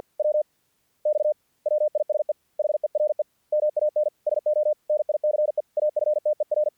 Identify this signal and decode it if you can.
Morse code, heard as a keyed tone, about 48 wpm, spaced as for 41 wpm, 600 Hz. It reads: WX WISE HERE MAN SO NICE AFTER
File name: archived / CRSnet